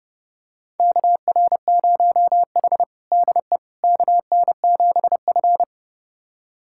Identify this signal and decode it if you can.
Morse code, heard as a keyed tone, about 30 wpm, 700 Hz. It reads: KR0H DE KN7F
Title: USA Callsign DE Callsign - 4 Characters - 30wpm